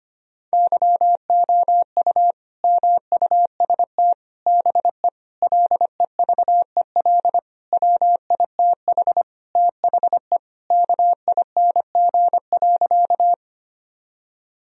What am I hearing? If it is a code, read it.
Morse code, heard as a keyed tone, about 25 wpm, 700 Hz. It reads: YOU MUST BE LEVEL WITH THE KING.
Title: Sentences from Top 1000 Words - Review Part 4 - 25wpm